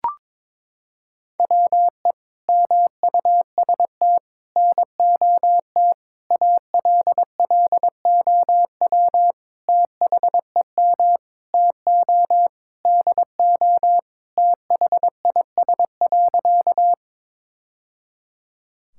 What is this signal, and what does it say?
Morse code, heard as a keyed tone, about 22 wpm, 700 Hz. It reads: WE MUST NOT ALLOW THEM TO DO THIS.
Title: Sentences from Top 1000 Words - Review Part 5 - 22wpm